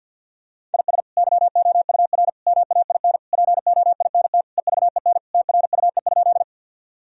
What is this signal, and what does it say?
Morse code, heard as a keyed tone, about 51 wpm, 700 Hz. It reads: IS YOUR MAIN POINT EVEN TRUE?
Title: Sentences from Top 1000 Words - Review Part 2 - 50wpm